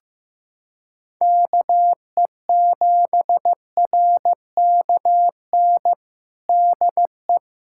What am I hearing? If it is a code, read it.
Morse code, heard as a keyed tone, about 16 wpm, 700 Hz. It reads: KE7RKN DE
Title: USA Callsign DE Callsign 15wpm